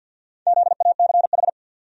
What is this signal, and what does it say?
Morse code, heard as a keyed tone, about 49 wpm, 700 Hz. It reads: ZACH
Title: USA Common Names - Max 5 Letters - 50wpm